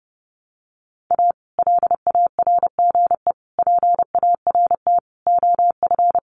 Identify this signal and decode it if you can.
Morse code, heard as a keyed tone, about 30 wpm, 700 Hz. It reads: A LARGE PART OF
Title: Sets of 5 Words Encore2 30wpm